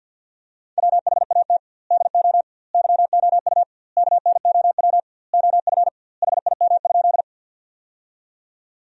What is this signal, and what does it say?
Morse code, heard as a keyed tone, about 50 wpm, 700 Hz. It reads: WHAT DO YOU KNOW OF HIM?
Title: 2x Word-Spacing - Sentences from Top 100 Words - 50wpm